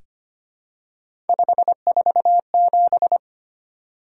Morse code, heard as a keyed tone, about 25 wpm, 700 Hz. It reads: 547
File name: Three Letter-Number Combinations - 25wpm